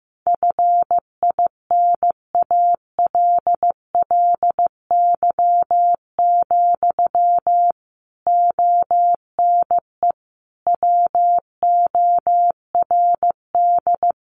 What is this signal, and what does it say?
Morse code, heard as a keyed tone, about 15 wpm, 700 Hz. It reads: FINALLY, ONE WORD
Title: Sentences from Top 1300 Words 15wpm